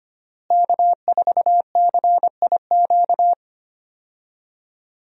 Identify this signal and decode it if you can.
Morse code, heard as a keyed tone, about 25 wpm, 700 Hz. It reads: K4CIQ
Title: USA Call Signs - 5 Characters - Repeated 3x - 25wpm